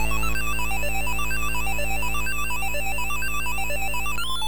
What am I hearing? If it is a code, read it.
There is no code.